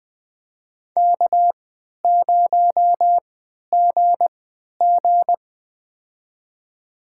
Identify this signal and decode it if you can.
Morse code, heard as a keyed tone, about 20 wpm, spaced as for 11 wpm, 700 Hz. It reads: K0GG
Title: USA Call Signs - 4 Characters - 3x Character Spacing - 20wpm